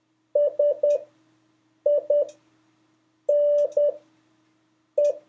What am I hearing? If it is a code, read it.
Morse code, heard as a keyed tone, about 10 wpm, spaced as for 6 wpm, 580 Hz. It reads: SINE
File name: 3of5-words-2